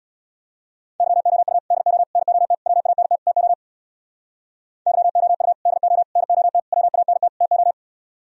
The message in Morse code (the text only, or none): YOU DO NOT WANT TO YOU DO NOT WANT TO